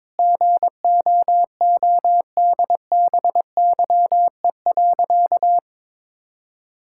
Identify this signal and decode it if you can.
Morse code, heard as a keyed tone, about 22 wpm, 700 Hz. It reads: GOODBYE.
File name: practice / 22wpm / Sentences with 4-Word Phrases - 22wpm